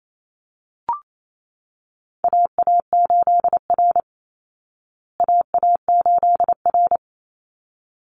Morse code, heard as a keyed tone, about 28 wpm, 700 Hz. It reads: AA8R AA8R
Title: Call Signs Repeated 3x - 28wpm